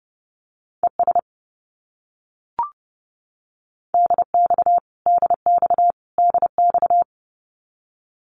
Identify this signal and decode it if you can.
Morse code, heard as a keyed tone, about 30 wpm, 700 Hz. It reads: ES DX DX DX
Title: Abbreviations Repeated 3x - 30wpm